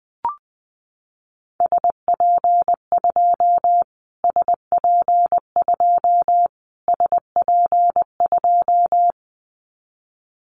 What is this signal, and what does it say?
Morse code, heard as a keyed tone, about 20 wpm, 700 Hz. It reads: SP2 SP2 SP2
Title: Callsign Prefixes for Top 25 DXCC - Repeated 3x - 20wpm